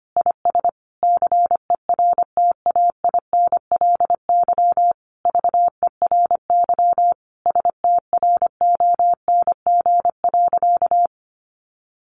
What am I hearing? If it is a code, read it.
Morse code, heard as a keyed tone, about 25 wpm, 700 Hz. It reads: IS CERTAINLY VERY STRONG.
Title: Sentences from Top 1300 Words 25wpm